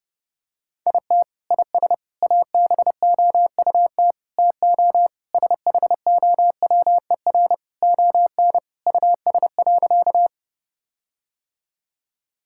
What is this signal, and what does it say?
Morse code, heard as a keyed tone, about 32 wpm, 700 Hz. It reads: IT IS ABOUT TO SHOWER ON US.
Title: Sentences from Top 1600 Words 30wpm